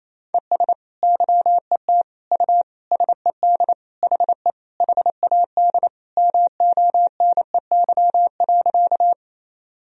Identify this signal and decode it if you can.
Morse code, heard as a keyed tone, about 28 wpm, 700 Hz. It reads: ES YET U SED HE HAD MONEY.
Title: Sentences from Top 500 Words and Abbreviations 28wpm